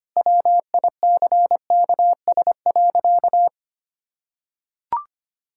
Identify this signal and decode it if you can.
Morse code, heard as a keyed tone, about 25 wpm, 700 Hz. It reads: WICKS.
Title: Pangram Challenge v2 - 25wpm